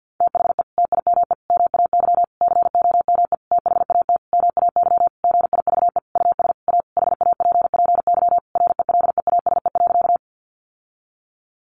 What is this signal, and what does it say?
Morse code, heard as a keyed tone, about 50 wpm, 700 Hz. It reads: THE TIME MAY COME THAT MAY GIVE US A HAPPY RELEASE.
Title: Sentences from Top 1100 Words 50wpm